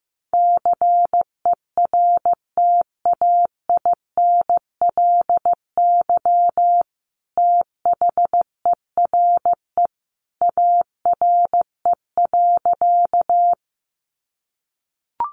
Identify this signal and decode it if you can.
Morse code, heard as a keyed tone, about 15 wpm, 700 Hz. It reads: CERTAINLY THERE ARE.
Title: Sentences from Top 900 Words 15wpm